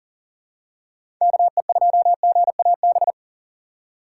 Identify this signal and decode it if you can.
Morse code, heard as a keyed tone, about 40 wpm, 700 Hz. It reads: KE2GAB
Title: USA Call Signs - 6 Characters - 4x Character Spacing - 40wpm